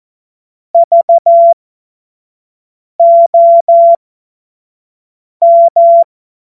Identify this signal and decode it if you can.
Morse code, heard as a keyed tone, about 14 wpm, spaced as for 5 wpm, 680 Hz. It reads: VOM
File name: Morsekurs / Witze / Witz 31. Januar 2023 25 F70